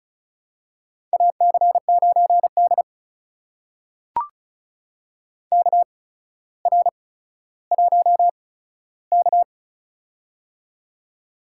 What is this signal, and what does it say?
Morse code, heard as a keyed tone, about 36 wpm, spaced as for 15 wpm, 700 Hz. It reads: AC9D KR1K
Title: USA Call Signs - 4 Characters - 8x Character Spacing - 35wpm